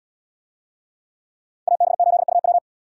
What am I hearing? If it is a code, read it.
Morse code, heard as a keyed tone, about 71 wpm, 700 Hz. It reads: AB8UW